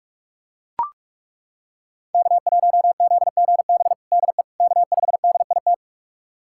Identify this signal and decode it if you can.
Morse code, heard as a keyed tone, about 41 wpm, 700 Hz. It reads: K1ZGB DE K5DIT
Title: USA Callsign DE Callsign - 5 Characters - 45wpm